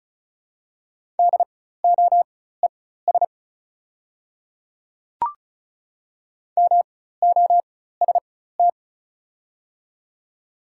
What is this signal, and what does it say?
Morse code, heard as a keyed tone, about 36 wpm, spaced as for 16 wpm, 690 Hz. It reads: DOES MOST
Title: 4x Character Spacing - Top 300 Words - 35wpm